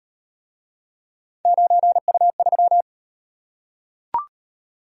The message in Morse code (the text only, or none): 9U3